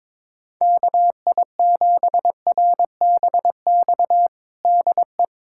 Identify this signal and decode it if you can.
Morse code, heard as a keyed tone, about 22 wpm, 700 Hz. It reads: KI7RBX DE
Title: USA Callsign DE Callsign - 6 Characters - 22wpm